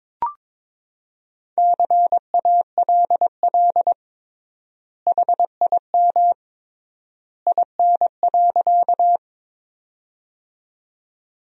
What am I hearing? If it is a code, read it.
Morse code, heard as a keyed tone, about 22 wpm, 700 Hz. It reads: CALL HIM IN.
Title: 3x Word-Spacing - Sentences from Top 100 Words - 22wpm